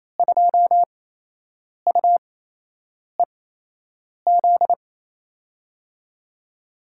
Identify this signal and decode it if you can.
Morse code, heard as a keyed tone, about 28 wpm, spaced as for 8 wpm, 700 Hz. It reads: 2UEZ
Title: USA Call Signs - 6 Characters - 8x Character Spacing - 28wpm